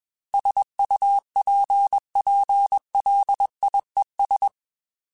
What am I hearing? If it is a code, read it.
Morse code, heard as a keyed tone, about 21 wpm, 800 Hz. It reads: SUPPLIES
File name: Technical Subjects / Technical Subjects-mp3s / Sw Pwr Sup @20